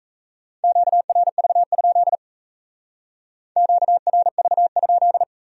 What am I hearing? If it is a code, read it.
Morse code, heard as a keyed tone, about 37 wpm, 700 Hz. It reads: QRV? QRV?